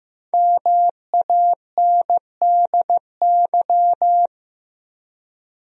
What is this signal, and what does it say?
Morse code, heard as a keyed tone, about 15 wpm, 700 Hz. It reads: MANDY
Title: USA Common Names - Max 5 Letters - 15wpm